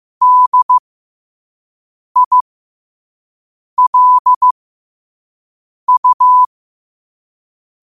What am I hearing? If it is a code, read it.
Morse code, heard as a keyed tone, about 15 wpm, spaced as for 5 wpm, 1000 Hz. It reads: DILU
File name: qotdfiles / 5wpm